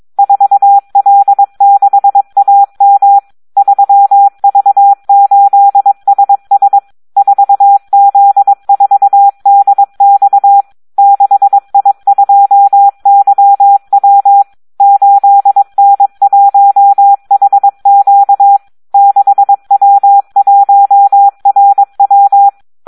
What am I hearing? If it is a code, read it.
Morse code, heard as a keyed tone, about 22 wpm, 800 Hz. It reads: 4L6AM 3V8SS 4Z4DX 6I2YW 8N1HQ 6W1RW
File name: Audio practica CW / SD 1 22